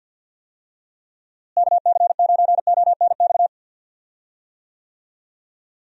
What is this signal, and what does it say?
Morse code, heard as a keyed tone, about 47 wpm, 700 Hz. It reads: KC9ONX
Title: USA Call Signs - 6 Characters - Repeated 3x - 50wpm